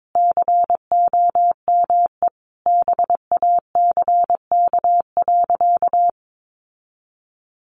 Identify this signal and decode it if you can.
Morse code, heard as a keyed tone, about 22 wpm, 700 Hz. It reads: COME BACK.
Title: Sentences from Top 1300 Words 22wpm